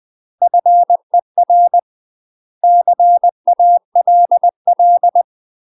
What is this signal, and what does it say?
Morse code, heard as a keyed tone, about 20 wpm, 700 Hz. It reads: FER CALL
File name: UK MASTER QSOs TRAINING/20WPM